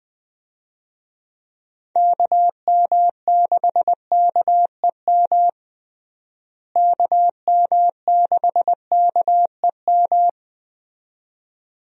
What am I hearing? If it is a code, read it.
Morse code, heard as a keyed tone, about 20 wpm, 700 Hz. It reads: KM6KEM KM6KEM
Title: USA Call Signs - 6 Characters - Repeated 2x - 20wpm